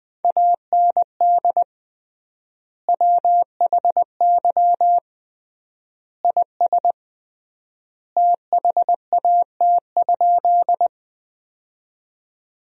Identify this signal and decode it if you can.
Morse code, heard as a keyed tone, about 20 wpm, 700 Hz. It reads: AND WHY IS THAT?
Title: 3x Word-Spacing - Sentences from Top 200 Words - 20wpm